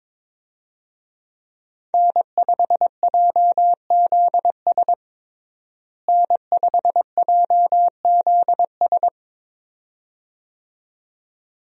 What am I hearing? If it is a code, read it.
Morse code, heard as a keyed tone, about 22 wpm, 700 Hz. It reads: N5JZS N5JZS
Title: USA Call Signs - 5 Characters - Repeated 2x - 22wpm